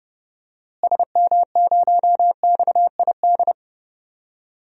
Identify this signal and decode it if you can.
Morse code, heard as a keyed tone, about 30 wpm, 700 Hz. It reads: SM0XID
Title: DXCC-100 Call Signs - 6 Characters - Repeated 2x - 30wpm